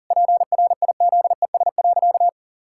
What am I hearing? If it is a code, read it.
Morse code, heard as a keyed tone, about 40 wpm, 700 Hz. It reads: PRIZES.
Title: Pangram Challenge v2 - 40wpm